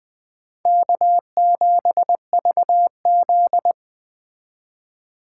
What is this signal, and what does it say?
Morse code, heard as a keyed tone, about 21 wpm, 700 Hz. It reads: K7VZ
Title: Speed Racing - USA 1x2 Call Signs 17wpm